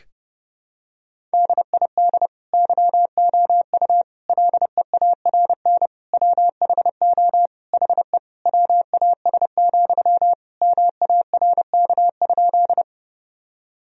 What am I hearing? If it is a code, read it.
Morse code, heard as a keyed tone, about 32 wpm, 700 Hz. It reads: DID YOU LEARN WHO HE WAS, MARK?
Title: Sentences from Top 1000 Words - Review Part 1 - 30wpm